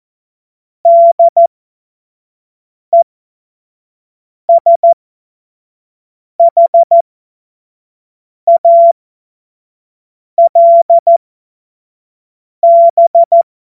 Morse code, heard as a keyed tone, about 14 wpm, spaced as for 5 wpm, 680 Hz. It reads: DESHALB
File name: Morsekurs / Witze / Witz 05. April 2022 25 F70